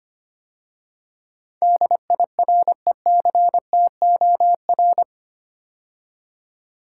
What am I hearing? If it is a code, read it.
Morse code, heard as a keyed tone, about 25 wpm, 700 Hz. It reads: DIRECTOR
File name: 8 Letter Words - First 3 Letters Given - 25wpm